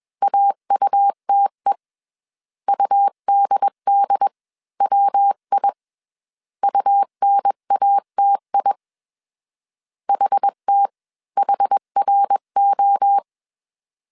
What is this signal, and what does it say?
Morse code, heard as a keyed tone, about 21 wpm, spaced as for 18 wpm, 800 Hz. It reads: AUTE UDDWI UNATI HTHRO
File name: CW / DET / 1 3 1500C S B 20wpm 1754